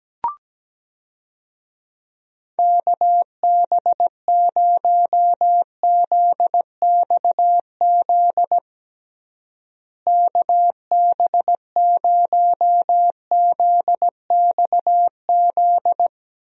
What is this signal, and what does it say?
Morse code, heard as a keyed tone, about 17 wpm, 700 Hz. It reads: KB0ZXZ KB0ZXZ
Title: USA Call Signs - 6 Characters - Repeated 3x - 17wpm